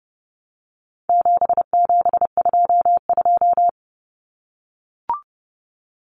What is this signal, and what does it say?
Morse code, heard as a keyed tone, about 30 wpm, 700 Hz. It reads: 7722
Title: Numbers - 4-Digit 30wpm